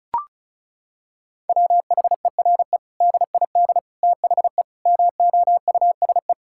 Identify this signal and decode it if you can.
Morse code, heard as a keyed tone, about 36 wpm, 700 Hz. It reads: WHERE DID THE MOUSE
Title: Sentences from Top 1000 Words - Review Part 5 - 35wpm